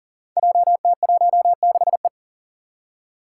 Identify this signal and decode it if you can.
Morse code, heard as a keyed tone, about 40 wpm, 700 Hz. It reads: JT16E
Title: Five Challenging Letter-Number Combinations 40wpm